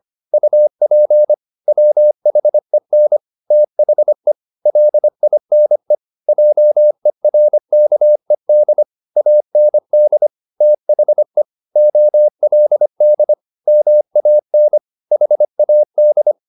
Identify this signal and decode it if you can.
Morse code, heard as a keyed tone, about 25 wpm, 600 Hz. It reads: UP WHEN THE LINE JERKED AND THE OLD MAN HAD
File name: the old man and the sea part 1 25wpm